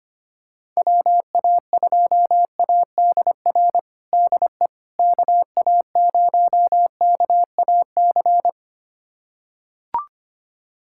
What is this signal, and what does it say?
Morse code, heard as a keyed tone, about 25 wpm, 700 Hz. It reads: WA2ADR DE KA0KAC
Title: USA Callsign DE Callsign - 6 Characters - 25wpm